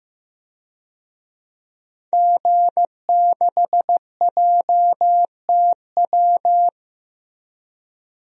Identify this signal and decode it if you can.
Morse code, heard as a keyed tone, about 16 wpm, 700 Hz. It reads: G6JTW
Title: DXCC-100 Call Signs - 5 Characters - Repeated 3x - 15wpm